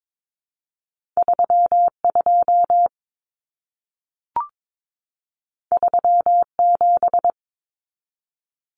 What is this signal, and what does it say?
Morse code, heard as a keyed tone, about 22 wpm, 700 Hz. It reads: 32 37
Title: Common Cut Numbers - 2-Digit 22wpm